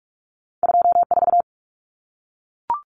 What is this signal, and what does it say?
Morse code, heard as a keyed tone, about 44 wpm, 700 Hz. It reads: 24
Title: Numbers - 2-Digit 45wpm